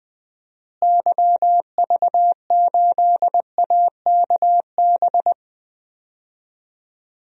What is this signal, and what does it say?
Morse code, heard as a keyed tone, about 20 wpm, 700 Hz. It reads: YV8AKB
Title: DXCC-100 Call Signs - 6 Characters - Repeated 3x - 20wpm